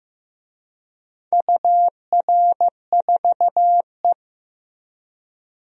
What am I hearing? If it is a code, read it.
Morse code, heard as a keyed tone, about 15 wpm, 700 Hz. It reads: UR4E
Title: DXCC-100 Call Signs - 4 Characters - Repeated 2x - 15wpm